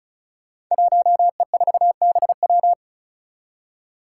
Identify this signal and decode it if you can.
Morse code, heard as a keyed tone, about 36 wpm, 700 Hz. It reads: 1E4BW
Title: Five Challenging Letter-Number Combinations 35wpm